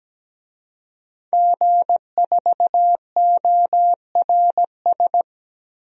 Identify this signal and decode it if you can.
Morse code, heard as a keyed tone, about 17 wpm, 700 Hz. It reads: G4ORS
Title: DXCC-100 Call Signs - 5 Characters - 17wpm